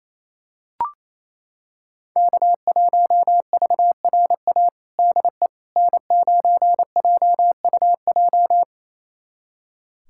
Morse code, heard as a keyed tone, about 28 wpm, 700 Hz. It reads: K1VRA DE N9JUJ
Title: USA Callsign DE Callsign - 5 Characters - 28wpm